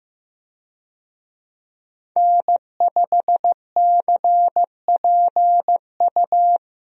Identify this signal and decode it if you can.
Morse code, heard as a keyed tone, about 15 wpm, 700 Hz. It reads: N5CPU
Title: USA Call Signs - 5 Characters - 15wpm